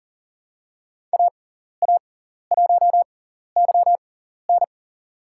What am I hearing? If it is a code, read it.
Morse code, heard as a keyed tone, about 40 wpm, spaced as for 14 wpm, 700 Hz. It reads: AA1YN
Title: USA Call Signs - 6x Character Spacing - 40wpm